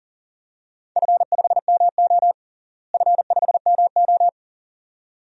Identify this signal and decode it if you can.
Morse code, heard as a keyed tone, about 40 wpm, 700 Hz. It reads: F5MO F5MO
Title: DXCC-100 Call Signs - 4 Characters - Repeated 2x - 40wpm